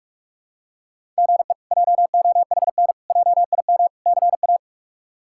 Morse code, heard as a keyed tone, about 46 wpm, 700 Hz. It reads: GE JOHN JIM CA
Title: K1USN SST QSO Practice - 45wpm